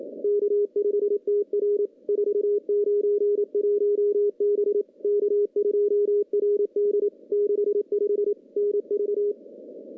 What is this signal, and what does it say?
Morse code, heard as a keyed tone, about 27 wpm, 410 Hz. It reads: K5TR 491 B K2RD 65 NV
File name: CW-SS-k5tr